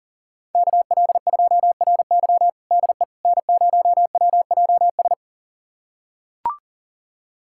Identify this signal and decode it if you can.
Morse code, heard as a keyed tone, about 40 wpm, 700 Hz. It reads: KL2RY DE N0WJS
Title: USA Callsign DE Callsign - 5 Characters - 40wpm